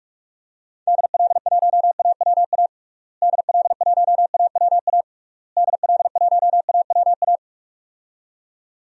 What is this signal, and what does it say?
Morse code, heard as a keyed tone, about 41 wpm, 700 Hz. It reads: DL1AWA DL1AWA DL1AWA
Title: DXCC-100 Call Signs - 6 Characters - Repeated 3x - 45wpm